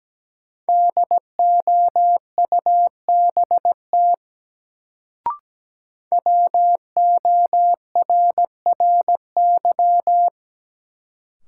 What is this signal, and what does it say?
Morse code, heard as a keyed tone, about 17 wpm, 700 Hz. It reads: DOUBT WORRY
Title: Sets of 5 Words - New Word Drill - 17wpm